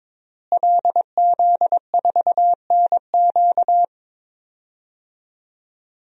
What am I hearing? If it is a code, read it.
Morse code, heard as a keyed tone, about 22 wpm, 700 Hz. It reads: LZ4NQ
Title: DXCC-100 Call Signs - 5 Characters - Repeated 3x - 22wpm